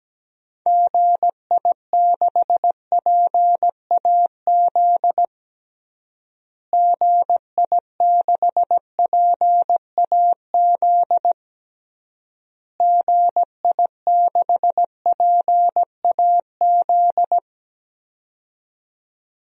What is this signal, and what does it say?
Morse code, heard as a keyed tone, about 17 wpm, 700 Hz. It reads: GI6PAZ GI6PAZ GI6PAZ
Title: DXCC-100 Call Signs - 6 Characters - Repeated 3x - 17wpm